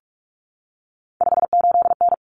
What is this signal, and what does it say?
Morse code, heard as a keyed tone, about 44 wpm, 700 Hz. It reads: 58N
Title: Common Cut Numbers - 3-Digit 45wpm